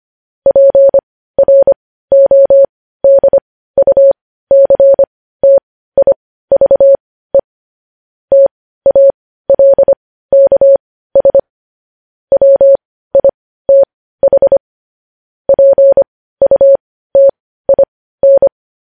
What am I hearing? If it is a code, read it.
Morse code, heard as a keyed tone, about 25 wpm, spaced as for 15 wpm, 550 Hz. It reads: PRODUCTIVE TALKS WITH PUTIN
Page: news-headlines-in-morse-code-at-15-wpm